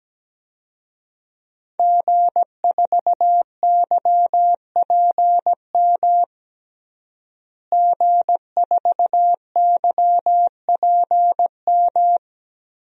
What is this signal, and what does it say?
Morse code, heard as a keyed tone, about 17 wpm, 700 Hz. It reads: G4YPM G4YPM
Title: DXCC-100 Call Signs - 5 Characters - Repeated 2x - 17wpm